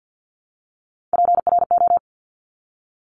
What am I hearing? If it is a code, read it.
Morse code, heard as a keyed tone, about 50 wpm, 700 Hz. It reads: PRO
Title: Top Three Letter Combinations in Words 50wpm